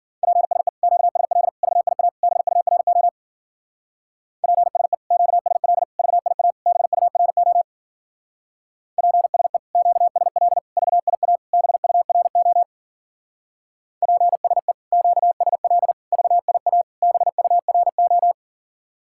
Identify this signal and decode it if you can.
Morse code, heard as a keyed tone, about 50 wpm, 700 Hz. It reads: PSE QSL VIA BURO PSE QSL VIA BURO PSE QSL VIA BURO PSE QSL VIA BURO